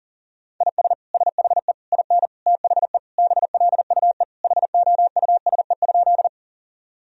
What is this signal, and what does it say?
Morse code, heard as a keyed tone, about 40 wpm, 700 Hz. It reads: IS SHE IN THE BLUE HOUSE?
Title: Sentences from Top 1000 Words - Review Part 3 - 40wpm